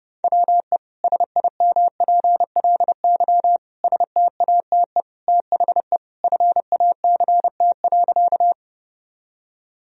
Morse code, heard as a keyed tone, about 29 wpm, 700 Hz. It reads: WE SIMPLY STATE THE FACT.
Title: Sentences from Top 1000 Words - Review Part 4 - 30wpm